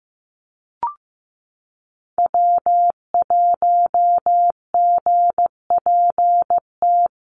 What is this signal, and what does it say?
Morse code, heard as a keyed tone, about 15 wpm, 700 Hz. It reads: W1GPT
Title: USA Call Signs Repeated 3x 15wpm